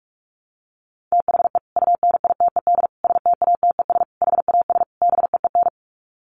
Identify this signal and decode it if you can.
Morse code, heard as a keyed tone, about 44 wpm, 700 Hz. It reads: THE UNITED STATES HAS BEEN
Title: Sets of 5 Words Encore2 45wpm